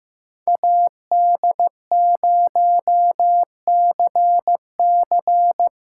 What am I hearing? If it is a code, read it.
Morse code, heard as a keyed tone, about 18 wpm, 700 Hz. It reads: AD0CC
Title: Speed Racing - USA Call Signs 15wpm